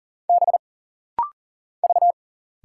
Morse code, heard as a keyed tone, about 40 wpm, 700 Hz. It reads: B V
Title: Single Letter-Number Alt - Rapid-Fire 40wpm